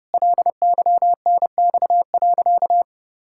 Morse code, heard as a keyed tone, about 30 wpm, 700 Hz. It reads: LYNX.
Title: Pangram Challenge v2 - 30wpm